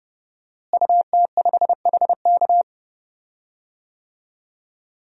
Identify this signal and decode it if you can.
Morse code, heard as a keyed tone, about 30 wpm, 700 Hz. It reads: UT5HK
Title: DXCC-100 Call Signs - 5 Characters - Repeated 2x - 30wpm